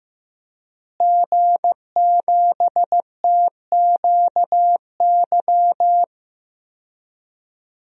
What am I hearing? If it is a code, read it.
Morse code, heard as a keyed tone, about 16 wpm, 700 Hz. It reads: G7TQY
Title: DXCC-100 Call Signs - 5 Characters - Repeated 3x - 15wpm